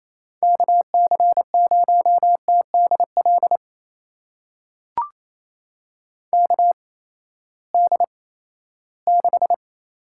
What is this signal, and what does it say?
Morse code, heard as a keyed tone, about 28 wpm, spaced as for 16 wpm, 700 Hz. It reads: KC0TDL KD6
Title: USA Call Signs - 8x Character Spacing - 28wpm